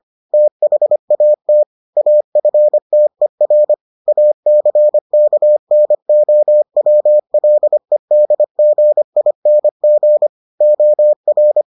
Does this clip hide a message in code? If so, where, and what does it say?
Yes: Morse code from 0.3 to 11.6 s, heard as a keyed tone, about 25 wpm, 600 Hz. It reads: THAT AFTER ACKNOWLEDGING OR